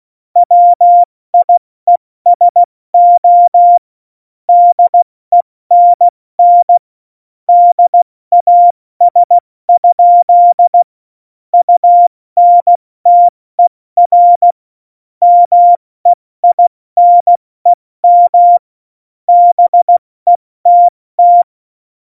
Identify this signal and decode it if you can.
Morse code, heard as a keyed tone, about 16 wpm, spaced as for 14 wpm, 700 Hz. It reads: WIESO DENN DAS? UNTER MEINEM BETT
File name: Morsekurs / Witze / Witz 01. September 2020, 70 F80